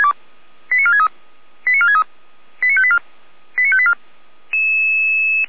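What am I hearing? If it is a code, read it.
Five-tone selective call (ZVEI), heard as quick runs of five tones, tones of about 70 ms. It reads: 89463 89463 89584 89584
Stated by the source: ZVEI five-tone selective call received by SDR.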